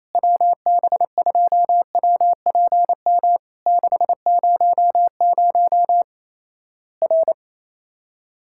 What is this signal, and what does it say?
Morse code, heard as a keyed tone, about 28 wpm, 700 Hz. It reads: WB2WPM 600 R
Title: Worked All Europe DX Contest - QTC Exchange - 28wpm